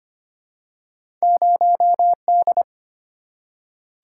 Morse code, heard as a keyed tone, about 25 wpm, 690 Hz. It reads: 0D
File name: Two Letter-Number Combinations - 25wpm